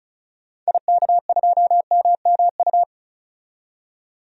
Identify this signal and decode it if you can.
Morse code, heard as a keyed tone, about 35 wpm, 700 Hz. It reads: IK2MMU
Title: DXCC-100 Call Signs - 6 Characters - 35wpm